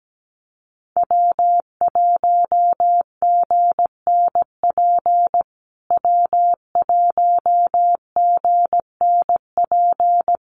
Morse code, heard as a keyed tone, about 17 wpm, 700 Hz. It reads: W1GNP W1GNP
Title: USA Call Signs Repeated 3x 17wpm